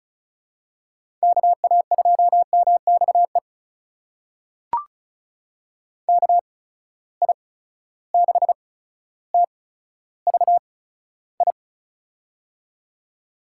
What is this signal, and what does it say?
Morse code, heard as a keyed tone, about 36 wpm, spaced as for 15 wpm, 700 Hz. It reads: KA2MXE KI6TVI
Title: USA Call Signs - 6 Characters - 8x Character Spacing - 35wpm